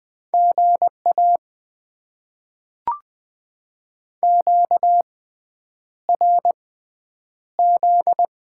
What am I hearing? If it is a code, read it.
Morse code, heard as a keyed tone, about 20 wpm, spaced as for 9 wpm, 700 Hz. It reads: GA QRZ
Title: Top 90 QSO Elements - 6x Character Spacing - 20wpm